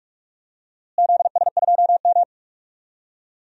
Morse code, heard as a keyed tone, about 46 wpm, 700 Hz. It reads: ZS2M
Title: DXCC-100 Call Signs - 4 Characters - Repeated 3x - 45wpm